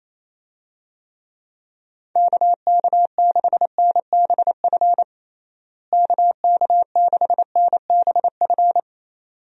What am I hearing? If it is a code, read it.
Morse code, heard as a keyed tone, about 28 wpm, 700 Hz. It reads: KK6NBF KK6NBF
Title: USA Call Signs - 6 Characters - Repeated 2x - 28wpm